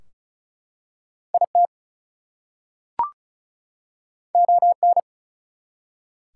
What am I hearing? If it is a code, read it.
Morse code, heard as a keyed tone, about 33 wpm, 700 Hz. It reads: IT ON
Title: Two Letter Words - 35wpm